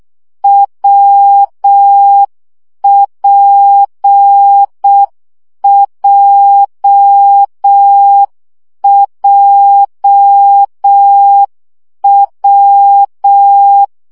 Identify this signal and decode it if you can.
Morse code, heard as a keyed tone, about 6 wpm, 800 Hz. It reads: WPJJW
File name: CW gyakorlas w p j 5 baud